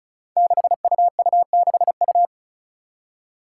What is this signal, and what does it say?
Morse code, heard as a keyed tone, about 36 wpm, 700 Hz. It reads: 6UU6U
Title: Five Challenging Letter-Number Combinations 35wpm